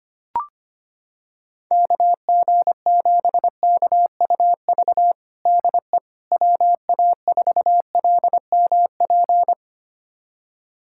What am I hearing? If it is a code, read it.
Morse code, heard as a keyed tone, about 25 wpm, 700 Hz. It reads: KG7KUV DE WA4LMP
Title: USA Callsign DE Callsign - 6 Characters - 25wpm